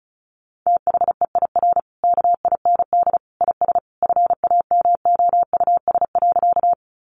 Morse code, heard as a keyed tone, about 34 wpm, 700 Hz. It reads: THEIR KIND IS FAMOUS.
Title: Sentences from Top 1500 Words 35wpm